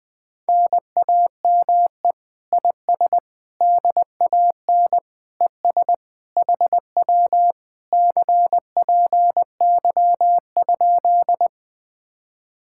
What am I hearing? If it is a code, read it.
Morse code, heard as a keyed tone, about 20 wpm, 700 Hz. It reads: NAME IS DAN ES HW CPY?
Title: Speed-Racing Common QSO Phrases Encore - 20wpm